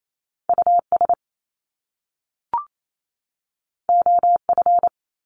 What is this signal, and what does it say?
Morse code, heard as a keyed tone, about 28 wpm, 700 Hz. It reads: US OF
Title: Top Two Letter Combinations in Words 28wpm